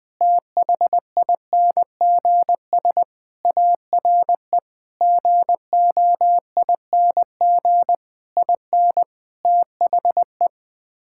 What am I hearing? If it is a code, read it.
Morse code, heard as a keyed tone, about 20 wpm, 700 Hz. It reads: THINGS ARE GOING IN THE
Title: Sets of 8 Words - 20wpm